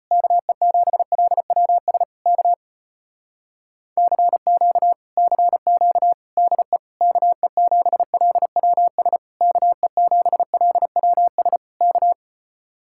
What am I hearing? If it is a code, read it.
Morse code, heard as a keyed tone, about 33 wpm, 700 Hz. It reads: KE7LWS K CQ CQ DE KE7LWS KE7LWS K